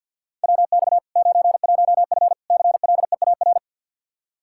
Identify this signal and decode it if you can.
Morse code, heard as a keyed tone, about 51 wpm, 700 Hz. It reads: WX 91F CLEAR
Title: K1USN SST QSO Practice 50wpm